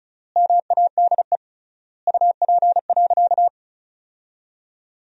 Morse code, heard as a keyed tone, about 36 wpm, 700 Hz. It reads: MADE UP.
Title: 3x Word-Spacing - Sentences from Top 500 Words - 35wpm